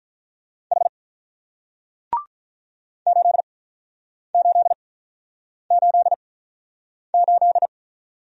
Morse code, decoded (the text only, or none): H 8888